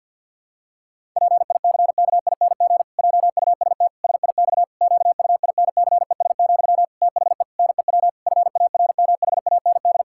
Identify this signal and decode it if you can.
Morse code, heard as a keyed tone, about 50 wpm, 700 Hz. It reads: PICKING JUST SIX QUINCES, THE NEW FARMHAND
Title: Pangram Challenge v2 - 50wpm